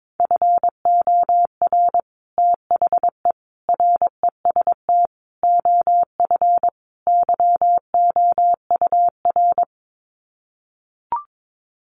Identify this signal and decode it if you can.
Morse code, heard as a keyed tone, about 22 wpm, 700 Hz. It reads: FOR THE REST OF YOUR
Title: Sets of 5 Words Encore 22wpm